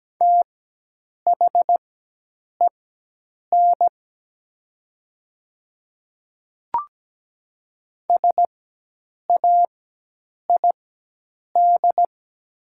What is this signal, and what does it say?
Morse code, heard as a keyed tone, about 17 wpm, spaced as for 8 wpm, 700 Hz. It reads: THEN SAID
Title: 4x Character Spacing - Top 300 Words - 17wpm